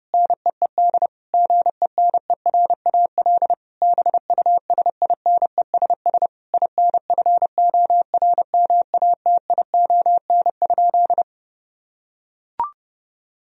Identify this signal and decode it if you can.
Morse code, heard as a keyed tone, about 30 wpm, 700 Hz. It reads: NEED GENERAL BUSINESS INFORMATION?
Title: Sentences from Top 1000 Words - Review Part 3 - 30wpm